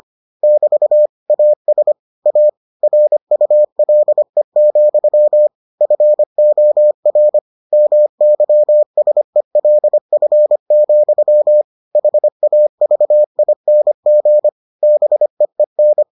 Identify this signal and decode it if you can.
Morse code, heard as a keyed tone, about 25 wpm, 600 Hz. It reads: <BT> AS A RULE, FOR MYSELF, HAVING BEEN